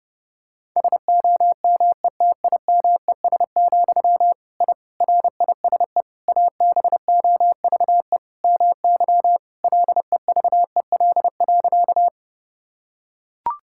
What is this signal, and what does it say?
Morse code, heard as a keyed tone, about 32 wpm, 700 Hz. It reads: SOMETIMES, I RISE ABOVE MY LEVEL.
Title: Sentences from Top 1000 Words - Review Part 6 - 30wpm